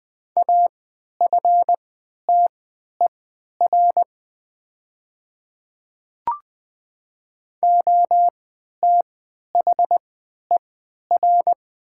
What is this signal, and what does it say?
Morse code, heard as a keyed tone, about 20 wpm, spaced as for 11 wpm, 700 Hz. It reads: AFTER OTHER
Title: 3x Character Spacing - Top 200 Words - 20wpm